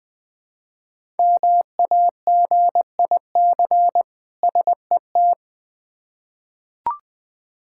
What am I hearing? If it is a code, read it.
Morse code, heard as a keyed tone, about 20 wpm, 700 Hz. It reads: MAGIC SET
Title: Difficult Words with Hints - 20wpm